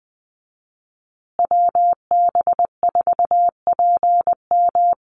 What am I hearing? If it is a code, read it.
Morse code, heard as a keyed tone, about 20 wpm, 700 Hz. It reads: WB4PM
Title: USA Call Signs Repeated 2x 20wpm